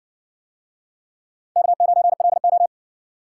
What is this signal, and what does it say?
Morse code, heard as a keyed tone, about 56 wpm, 700 Hz. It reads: K9LO